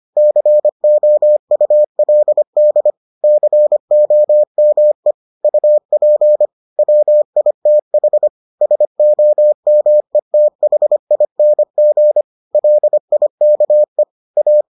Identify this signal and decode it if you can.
Morse code, heard as a keyed tone, about 25 wpm, 600 Hz. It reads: COULD COME UP WITH SOMETHING LIKE A